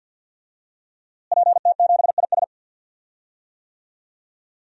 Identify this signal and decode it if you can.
Morse code, heard as a keyed tone, about 49 wpm, 700 Hz. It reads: PT7IS
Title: DXCC-100 Call Signs - 5 Characters - Repeated 2x - 50wpm